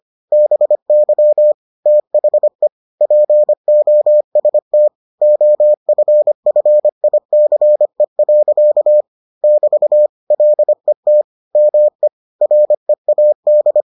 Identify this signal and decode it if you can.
Morse code, heard as a keyed tone, about 25 wpm, 600 Hz. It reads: BY THE POST OFFICE. <BT> LET ME READ